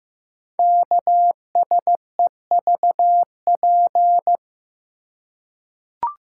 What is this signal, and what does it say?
Morse code, heard as a keyed tone, about 15 wpm, 700 Hz. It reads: KSEVP
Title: Code Groups - 5 Letter:Numbers - 15wpm